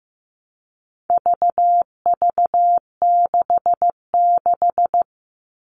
Morse code, heard as a keyed tone, about 15 wpm, 700 Hz. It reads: VV66
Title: All Cut Numbers - 4-Digit 15wpm